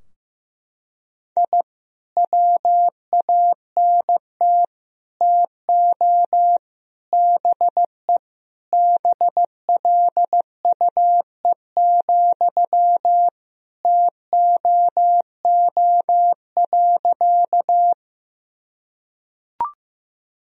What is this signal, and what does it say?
Morse code, heard as a keyed tone, about 15 wpm, 700 Hz. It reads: I WANT TO BE BLUE, TOO.
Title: Sentences from Top 1000 Words - Review Part 3 - 15wpm